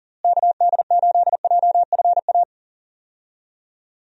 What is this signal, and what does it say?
Morse code, heard as a keyed tone, about 40 wpm, 700 Hz. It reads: KD8JFA
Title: USA Call Signs - 6 Characters - Repeated 3x - 40wpm